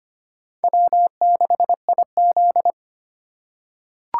Morse code, heard as a keyed tone, about 25 wpm, 700 Hz. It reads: W6IZ
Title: USA Call Signs - 3x Character Spacing - 25wpm